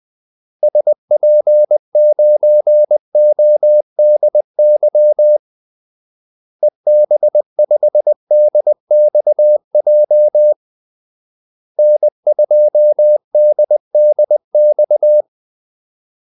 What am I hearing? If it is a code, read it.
Morse code, heard as a keyed tone, about 20 wpm, 600 Hz. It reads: SP9ODY EB5DXJ N2DDX
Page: Kaldesignaler 20 wpm